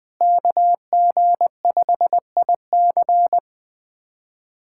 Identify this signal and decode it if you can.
Morse code, heard as a keyed tone, about 20 wpm, 700 Hz. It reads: KG5IC
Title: USA Callsign DE Callsign - 5 Characters - 20wpm